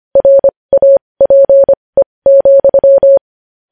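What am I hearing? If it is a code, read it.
Morse code, heard as a keyed tone, about 25 wpm, spaced as for 20 wpm, 550 Hz. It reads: RAPE,
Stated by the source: Morse news headlines sent at 20.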